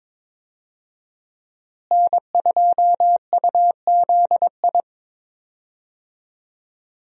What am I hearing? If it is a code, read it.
Morse code, heard as a keyed tone, about 22 wpm, 700 Hz. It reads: N2UZI
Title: USA Call Signs - 5 Characters - Repeated 2x - 22wpm